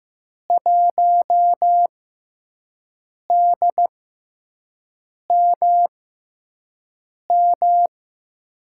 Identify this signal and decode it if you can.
Morse code, heard as a keyed tone, about 15 wpm, spaced as for 5 wpm, 700 Hz. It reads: 1DMM
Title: USA Call Signs - 6 Characters - 6x Character Spacing - 15wpm